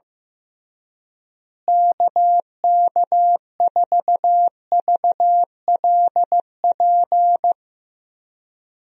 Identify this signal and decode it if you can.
Morse code, heard as a keyed tone, about 15 wpm, 700 Hz. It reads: KK4VLP
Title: USA Call Signs - 6 Characters - Repeated 3x - 15wpm